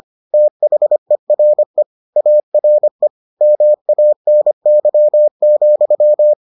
Morse code, heard as a keyed tone, about 25 wpm, 600 Hz. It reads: THERE ARE MANY,